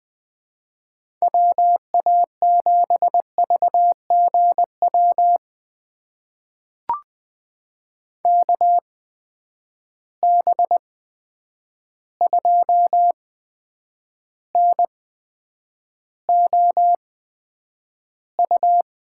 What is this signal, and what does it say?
Morse code, heard as a keyed tone, about 20 wpm, spaced as for 9 wpm, 700 Hz. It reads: WA7VGW KB2NOU
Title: USA Call Signs - 6 Characters - 8x Character Spacing - 20wpm